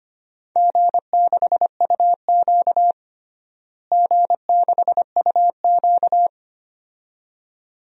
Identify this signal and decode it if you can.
Morse code, heard as a keyed tone, about 25 wpm, 700 Hz. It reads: G6UQ G6UQ
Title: DXCC-100 Call Signs - 4 Characters - Repeated 2x - 25wpm